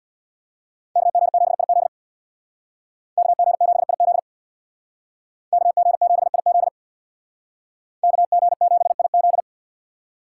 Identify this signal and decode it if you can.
Morse code, heard as a keyed tone, about 63 wpm, 700 Hz. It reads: KG7IZ KG7IZ KG7IZ KG7IZ